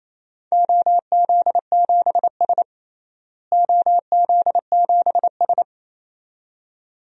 Morse code, heard as a keyed tone, about 28 wpm, 700 Hz. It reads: OZ7S OZ7S
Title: DXCC-100 Call Signs - 4 Characters - Repeated 2x - 28wpm